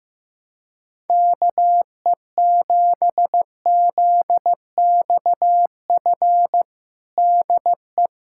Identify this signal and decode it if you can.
Morse code, heard as a keyed tone, about 16 wpm, 700 Hz. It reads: KE7ZXF DE
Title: USA Callsign DE Callsign - 6 Characters - 15wpm